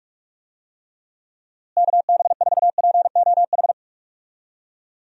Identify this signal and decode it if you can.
Morse code, heard as a keyed tone, about 44 wpm, 700 Hz. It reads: KB4POH
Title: USA Call Signs - 6 Characters - Repeated 3x - 45wpm